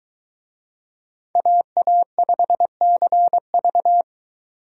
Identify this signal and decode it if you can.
Morse code, heard as a keyed tone, about 23 wpm, 700 Hz. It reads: AA5CV